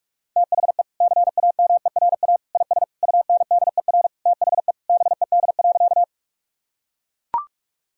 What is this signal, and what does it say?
Morse code, heard as a keyed tone, about 41 wpm, 700 Hz. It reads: THE CAMERA IS UNDER THE BED.
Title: Sentences from Top 1000 Words - Review Part 5 - 45wpm